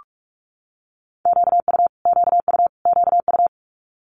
Morse code, heard as a keyed tone, about 46 wpm, 700 Hz. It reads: QU QU QU
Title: Multi-letter Phonemes Repeated 3x 45wpm